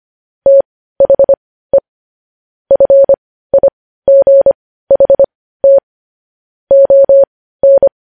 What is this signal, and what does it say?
Morse code, heard as a keyed tone, about 25 wpm, spaced as for 15 wpm, 550 Hz. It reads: THE FIGHT ON
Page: news-headlines-in-morse-code-at-15-wpm